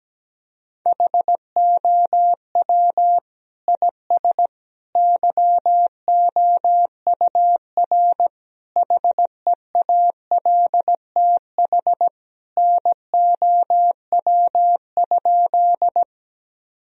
Morse code, heard as a keyed tone, about 17 wpm, 700 Hz. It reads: HOW IS YOUR HEALTH NOW?
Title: Sentences from Top 1000 Words - Review Part 3 - 17wpm